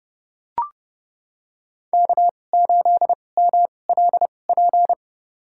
Morse code, heard as a keyed tone, about 31 wpm, spaced as for 22 wpm, 700 Hz. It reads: K8MLP
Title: USA Call Signs - 5 Characters - 2x Character Spacing - 30wpm